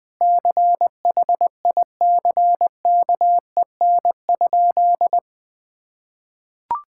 Morse code, heard as a keyed tone, about 20 wpm, 700 Hz. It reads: CHICKEN?
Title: Sentences from Top 1000 Words - Review Part 5 - 20wpm